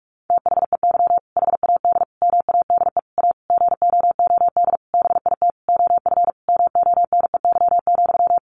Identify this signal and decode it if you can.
Morse code, heard as a keyed tone, about 44 wpm, 700 Hz. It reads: THEY HAD MADE A GOOD BIT OF MONEY,
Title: Sentences from Top 900 Words 45wpm